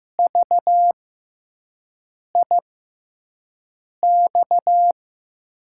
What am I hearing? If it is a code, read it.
Morse code, heard as a keyed tone, about 15 wpm, spaced as for 5 wpm, 700 Hz. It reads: VIX
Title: USA Call Signs - 6 Characters - 6x Character Spacing - 15wpm